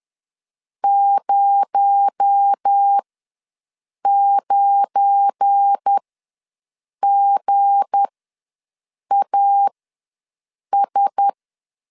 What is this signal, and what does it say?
Morse code, heard as a keyed tone, about 10 wpm, spaced as for 6 wpm, 800 Hz. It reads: 09GAS
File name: LEZIONI / 7 GF9Z / 7 GF9 ws 10wpm 4034